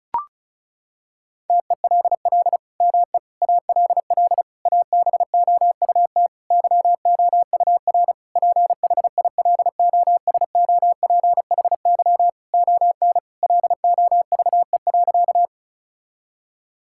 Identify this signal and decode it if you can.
Morse code, heard as a keyed tone, about 34 wpm, 700 Hz. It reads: TELL ME ALL ABOUT YOUR PHILOSOPHY ON LOVE.
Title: Sentences from Top 1000 Words - Review Part 6 - 35wpm